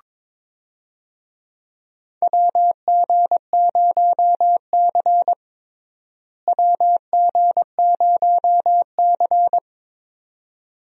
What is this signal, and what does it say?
Morse code, heard as a keyed tone, about 22 wpm, 700 Hz. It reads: WG0C WG0C
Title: USA Call Signs - 4 Characters - Repeated 2x - 22wpm